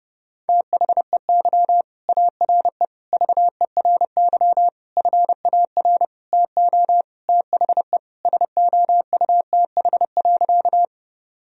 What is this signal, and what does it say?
Morse code, heard as a keyed tone, about 28 wpm, 700 Hz. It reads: THEY ARE VERY FAR TO THE SOUTH.
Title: Sentences from Top 1000 Words - Review Part 2 - 30wpm